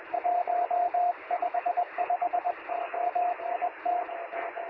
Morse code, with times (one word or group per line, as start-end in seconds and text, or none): none